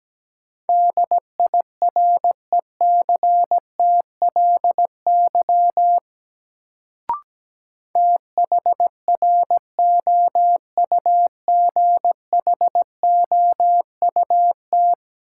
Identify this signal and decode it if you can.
Morse code, heard as a keyed tone, about 17 wpm, 700 Hz. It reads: DIRECTLY THROUGHOUT
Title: Sentences from Top 800 Words - New Word Drill - 17wpm